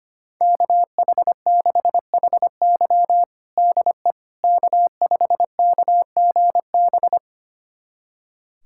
Morse code, heard as a keyed tone, about 25 wpm, 700 Hz. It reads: KH6HY DE K5KGB
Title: USA Callsign DE Callsign - 5 Characters - 25wpm